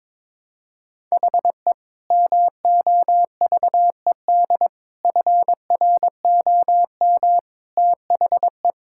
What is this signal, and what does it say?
Morse code, heard as a keyed tone, about 21 wpm, 700 Hz. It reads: HE MOVED FROM THE
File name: practice / 22wpm / Sentences with 4-Word Phrases - 22wpm